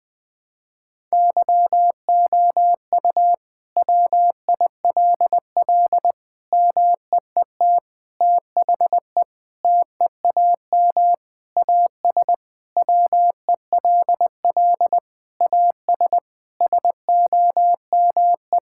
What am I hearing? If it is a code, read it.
Morse code, heard as a keyed tone, about 20 wpm, 700 Hz. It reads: YOU WILL MEET THE TEAM AS WELL AS SOME
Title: Sentences with 6-Word Phrases - 20wpm